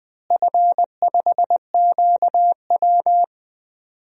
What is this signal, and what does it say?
Morse code, heard as a keyed tone, about 20 wpm, 700 Hz. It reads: F5QW
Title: DXCC-100 Call Signs - 4 Characters - Repeated 3x - 20wpm